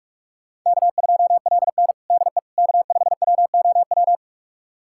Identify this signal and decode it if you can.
Morse code, heard as a keyed tone, about 46 wpm, 700 Hz. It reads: K2LN DE K5WOW
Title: USA Callsign DE Callsign 45wpm